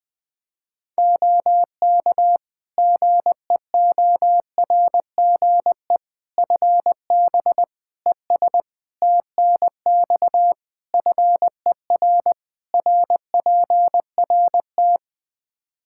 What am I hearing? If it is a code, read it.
Morse code, heard as a keyed tone, about 20 wpm, 700 Hz. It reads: OK GEORGE FB ES TNX FER RPRT
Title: K1USN SST QSO Practice 20wpm